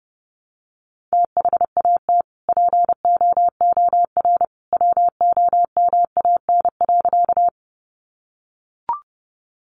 Morse code, heard as a keyed tone, about 30 wpm, 700 Hz. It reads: THAT POOR WOMAN.
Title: Sentences from Top 1100 Words 30wpm